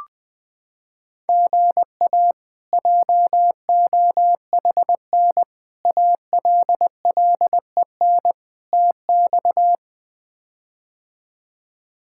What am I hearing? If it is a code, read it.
Morse code, heard as a keyed tone, about 21 wpm, 700 Hz. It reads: GA JOHN ALLEN TX
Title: K1USN SST QSO Practice - 20wpm